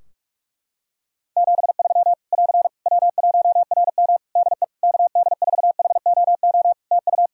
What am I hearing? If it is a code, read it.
Morse code, heard as a keyed tone, about 42 wpm, 700 Hz. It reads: 73 <AR> W1RM DE KD4HOO TU
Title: Practice QSO - Exchange 4 - Ending - 45wpm